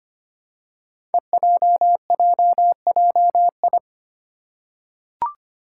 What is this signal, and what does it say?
Morse code, heard as a keyed tone, about 25 wpm, 700 Hz. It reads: EJJJI
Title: Five Challenging Letter-Number Combinations 25wpm